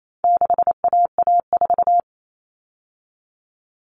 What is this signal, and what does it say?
Morse code, heard as a keyed tone, about 28 wpm, 700 Hz. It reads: 6AA4
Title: All Cut Numbers - 4-Digit 28wpm